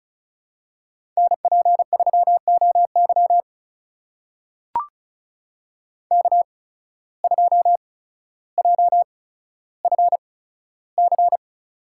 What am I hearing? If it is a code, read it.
Morse code, heard as a keyed tone, about 34 wpm, spaced as for 15 wpm, 700 Hz. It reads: NP3OY K2JFC
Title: USA Call Signs - 5 Characters - 8x Character Spacing - 35wpm